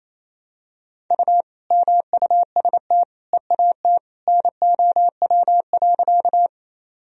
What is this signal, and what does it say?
Morse code, heard as a keyed tone, about 28 wpm, 700 Hz. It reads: U MUST EAT NOW.
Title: Sentences from Top 300 Words and Abbreviations 28wpm